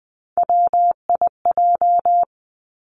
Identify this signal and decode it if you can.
Morse code, heard as a keyed tone, about 20 wpm, 700 Hz. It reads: WIJ
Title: Three Challenging Letter-Number Combinations 20wpm